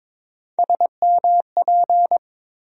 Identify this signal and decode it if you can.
Morse code, heard as a keyed tone, about 22 wpm, 700 Hz. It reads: SMP
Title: USA Call Sign Postfixes - 3 Letters 22wpm